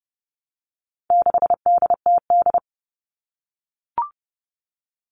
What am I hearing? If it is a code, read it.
Morse code, heard as a keyed tone, about 30 wpm, 700 Hz. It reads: 6DTD
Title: All Cut Numbers - 4-Digit 30wpm